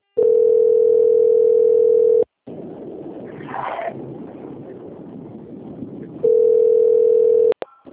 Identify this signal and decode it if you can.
Ringback tone (North America), 440 + 480 Hz.